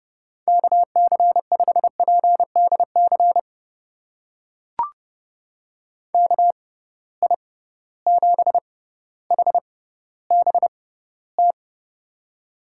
Morse code, heard as a keyed tone, about 30 wpm, spaced as for 15 wpm, 700 Hz. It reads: KC5PDC KI7HBT
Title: USA Call Signs - 6 Characters - 6x Character Spacing - 30wpm